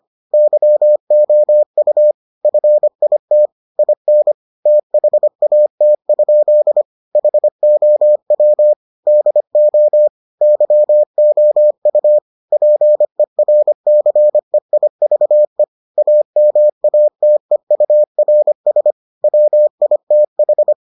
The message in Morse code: YOU FIT IN THAT? HOW DO YOU PERCEIVE AMATEURS WITH

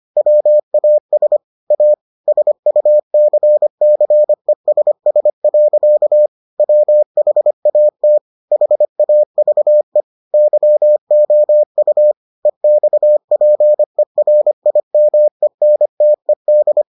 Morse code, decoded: WAS A SUCCESS. WHAT HAVE YOU EXPERIMENTED

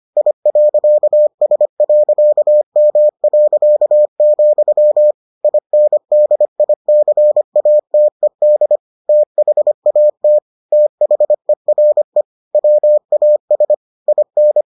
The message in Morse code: I.S.M., INDICATED THAT THERE WAS IN

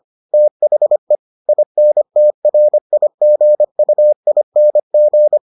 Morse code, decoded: THE INTRIGUING

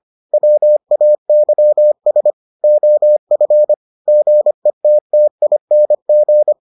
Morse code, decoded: WAYS OF GETTING

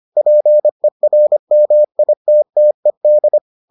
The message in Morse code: PERMITTED

